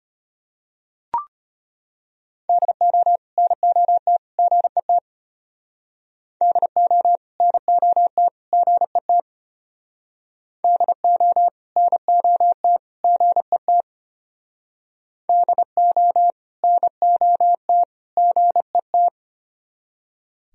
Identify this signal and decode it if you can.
Morse code, heard as a keyed tone, about 30 wpm, 700 Hz. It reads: DO NOT GET DO NOT GET DO NOT GET DO NOT GET
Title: Speed-Racing - Sets of 3 Words 25wpm